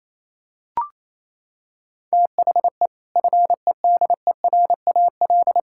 Morse code, heard as a keyed tone, about 28 wpm, 700 Hz. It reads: THE FEDERAL
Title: Sentences from Top 1600 Words 28wpm